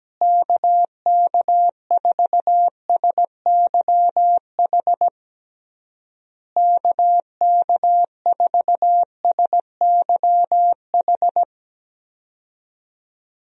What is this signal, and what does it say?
Morse code, heard as a keyed tone, about 17 wpm, 700 Hz. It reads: KK4SYH KK4SYH
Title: USA Call Signs - 6 Characters - Repeated 3x - 17wpm